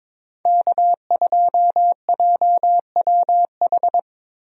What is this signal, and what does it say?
Morse code, heard as a keyed tone, about 21 wpm, 700 Hz. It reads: K2JWH
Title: USA Call Signs - 5 Characters - Repeated 2x - 22wpm